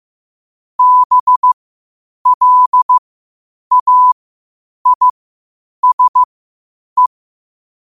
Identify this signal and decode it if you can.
Morse code, heard as a keyed tone, about 15 wpm, spaced as for 8 wpm, 1000 Hz. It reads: BLAISE
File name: qotdfiles / 8wpm